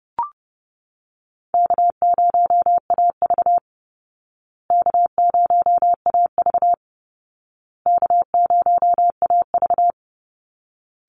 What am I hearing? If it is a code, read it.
Morse code, heard as a keyed tone, about 30 wpm, 700 Hz. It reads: K0AV K0AV K0AV
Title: Call Signs Repeated 2x - 30wpm